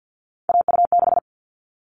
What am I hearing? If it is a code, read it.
Morse code, heard as a keyed tone, about 50 wpm, 700 Hz. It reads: AU6
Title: All Cut Numbers - 3-Digit 50wpm